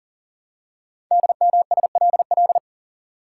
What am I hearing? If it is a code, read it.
Morse code, heard as a keyed tone, about 39 wpm, 700 Hz. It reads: DMSLL
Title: Code Groups - 5 Letter:Numbers - 40wpm